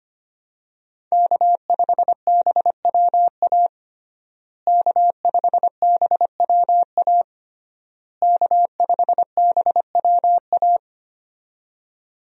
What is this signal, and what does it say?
Morse code, heard as a keyed tone, about 25 wpm, 700 Hz. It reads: K5BWA K5BWA K5BWA
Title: USA Call Signs - 5 Characters - Repeated 3x - 25wpm